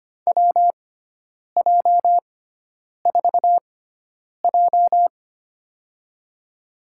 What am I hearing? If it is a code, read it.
Morse code, heard as a keyed tone, about 25 wpm, spaced as for 9 wpm, 700 Hz. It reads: WJ4J
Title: USA Call Signs - 4 Characters - 6x Character Spacing - 25wpm